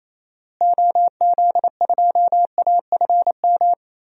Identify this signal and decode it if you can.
Morse code, heard as a keyed tone, about 28 wpm, 700 Hz. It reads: OZ2AFM
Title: DXCC-100 Call Signs - 6 Characters - Repeated 2x - 28wpm